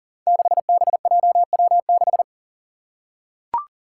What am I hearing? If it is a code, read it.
Morse code, heard as a keyed tone, about 41 wpm, 700 Hz. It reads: 6BJW6
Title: Five Challenging Letter-Number Combinations 40wpm